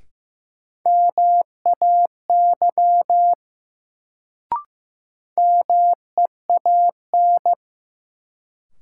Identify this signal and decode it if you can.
Morse code, heard as a keyed tone, about 15 wpm, 700 Hz. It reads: MAY MEAN
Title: Sets of 2 Words - New Word Drill - 15wpm